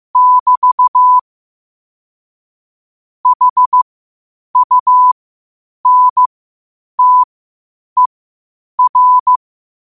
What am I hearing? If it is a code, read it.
Morse code, heard as a keyed tone, about 15 wpm, spaced as for 8 wpm, 1000 Hz. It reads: <BT> HUNTER
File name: qotdfiles / 8wpm